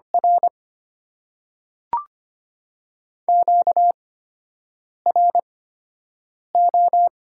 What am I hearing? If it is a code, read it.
Morse code, heard as a keyed tone, about 25 wpm, spaced as for 7 wpm, 690 Hz. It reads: R QRO
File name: Top 90 QSO Elements - 8x Character Spacing - 25wpm